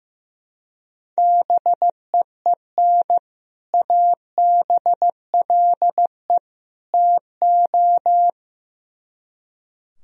Morse code, heard as a keyed tone, about 15 wpm, 700 Hz. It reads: BEEN ABLE TO
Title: Speed-Racing - Sets of 3 Words 15wpm